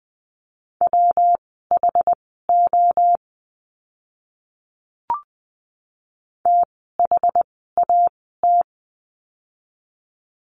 Morse code, heard as a keyed tone, about 20 wpm, spaced as for 15 wpm, 700 Hz. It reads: WHO THAT
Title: 244 Lesson 39f - Letter z - Words 20wpm